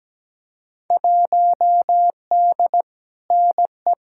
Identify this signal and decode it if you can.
Morse code, heard as a keyed tone, about 17 wpm, 700 Hz. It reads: 1D NE
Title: Field Day Exchanges - 17wpm